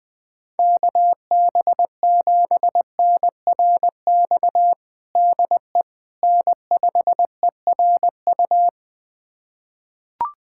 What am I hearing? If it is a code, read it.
Morse code, heard as a keyed tone, about 20 wpm, 700 Hz. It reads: KB7NRX DE N5ERU
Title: USA Callsign DE Callsign 20wpm